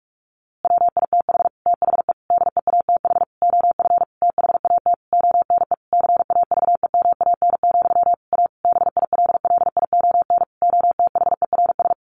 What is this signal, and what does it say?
Morse code, heard as a keyed tone, about 44 wpm, 700 Hz. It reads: WITH THE DEATH OF THAT ONE CAVEMAN, A BILLION OTHERS
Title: A Sound Of Thunder 45wpm